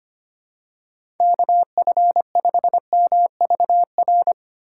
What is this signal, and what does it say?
Morse code, heard as a keyed tone, about 25 wpm, 700 Hz. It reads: KF5MVR
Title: USA Call Signs - 6 Characters - 4x Character Spacing - 25wpm